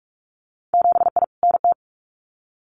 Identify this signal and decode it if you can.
Morse code, heard as a keyed tone, about 46 wpm, 700 Hz. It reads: 7I NT
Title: Winter Field Day Exchanges Encore 45wpm